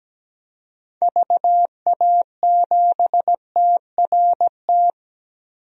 Morse code, heard as a keyed tone, about 17 wpm, 700 Hz. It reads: VA7TRT
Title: DXCC-100 Call Signs - 6 Characters - 17wpm